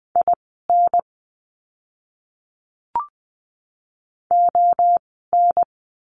Morse code, heard as a keyed tone, about 20 wpm, spaced as for 15 wpm, 700 Hz. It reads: IN ON
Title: 061 Lesson 11f - Letter h - Words 20wpm